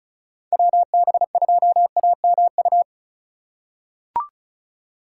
Morse code, heard as a keyed tone, about 34 wpm, 700 Hz. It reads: WB2AMU
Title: USA Call Signs - 4x Character Spacing - 35wpm